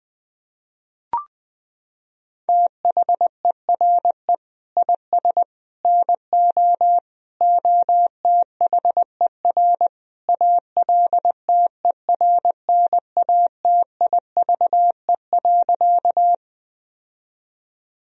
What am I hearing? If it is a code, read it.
Morse code, heard as a keyed tone, about 20 wpm, 700 Hz. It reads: THERE IS NO OTHER ALTERNATIVE.
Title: Sentences from Top 1000 Words - Review Part 4 - 20wpm